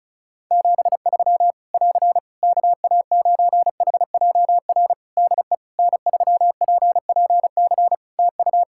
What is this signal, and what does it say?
Morse code, heard as a keyed tone, about 36 wpm, 700 Hz. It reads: 73 <AR> KA9HJR DE N3PPC TU
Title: Practice QSO - Exchange 4 - Ending - 35wpm